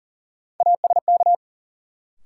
Morse code, heard as a keyed tone, about 40 wpm, 700 Hz. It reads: ASK
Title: Sentences from Top 200 Words - New Word Drill - 40wpm